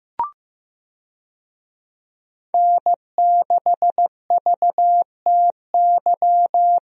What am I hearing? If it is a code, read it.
Morse code, heard as a keyed tone, about 16 wpm, 700 Hz. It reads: N6VTY
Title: USA Call Signs - 5 Characters - Repeated 3x - 15wpm